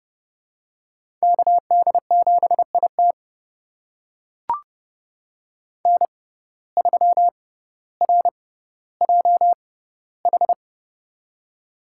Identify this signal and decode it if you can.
Morse code, heard as a keyed tone, about 30 wpm, spaced as for 15 wpm, 700 Hz. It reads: KD7IT N3RJH
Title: USA Call Signs - 5 Characters - 6x Character Spacing - 30wpm